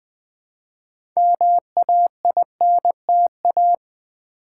Morse code, heard as a keyed tone, about 20 wpm, 700 Hz. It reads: MAINTA
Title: Build A Word - 3 Letters At A Time - 8-Letter Words - 20wpm